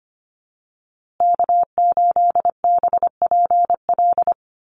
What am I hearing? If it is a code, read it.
Morse code, heard as a keyed tone, about 25 wpm, 700 Hz. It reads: K8BPL
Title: USA Call Signs Repeated 2x 25wpm